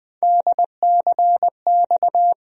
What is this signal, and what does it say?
Morse code, heard as a keyed tone, about 21 wpm, 700 Hz. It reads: DCX
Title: USA Call Sign Postfixes - 3 Letters 20wpm